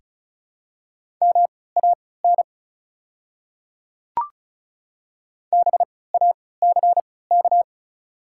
Morse code, heard as a keyed tone, about 36 wpm, spaced as for 20 wpm, 700 Hz. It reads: MAN BACK
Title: 3x Character Spacing - Top 200 Words - 35wpm